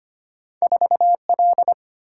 Morse code, heard as a keyed tone, about 25 wpm, 700 Hz. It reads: 4L
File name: Two Letter-Number Combinations - 25wpm